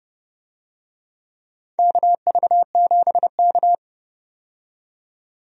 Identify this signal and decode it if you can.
Morse code, heard as a keyed tone, about 30 wpm, 700 Hz. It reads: KV7K
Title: USA Call Signs - 4 Characters - 30wpm